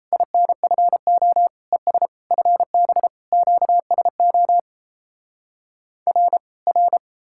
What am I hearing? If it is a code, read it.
Morse code, heard as a keyed tone, about 38 wpm, 700 Hz. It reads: INFO ES FB QSO R R